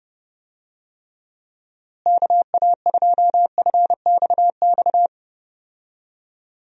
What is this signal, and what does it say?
Morse code, heard as a keyed tone, about 29 wpm, 700 Hz. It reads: KA2FXX
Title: USA Call Signs - 6 Characters - Repeated 3x - 30wpm